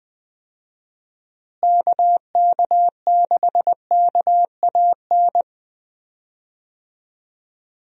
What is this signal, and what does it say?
Morse code, heard as a keyed tone, about 20 wpm, 700 Hz. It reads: KK6KAN
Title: USA Call Signs - 6 Characters - Repeated 2x - 20wpm